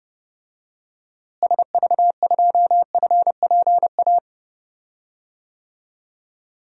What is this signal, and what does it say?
Morse code, heard as a keyed tone, about 31 wpm, 700 Hz. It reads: SV2FPA
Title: DXCC-100 Call Signs - 6 Characters - Repeated 3x - 30wpm